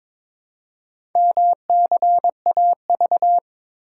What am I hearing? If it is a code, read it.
Morse code, heard as a keyed tone, about 22 wpm, 700 Hz. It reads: MCAV
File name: practice / 22wpm / Four Letter-Number Combinations - 22wpm